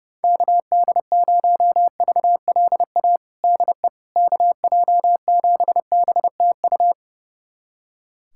Morse code, heard as a keyed tone, about 30 wpm, 700 Hz. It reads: KD0VLA DE KJ7BTU
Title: USA Callsign DE Callsign - 6 Characters - 30wpm